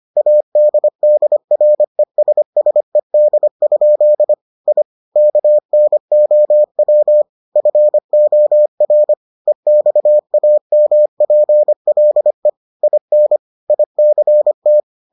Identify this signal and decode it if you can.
Morse code, heard as a keyed tone, about 25 wpm, 600 Hz. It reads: ADDRESSED? I KNOW FOR EXAMPLE IN ICT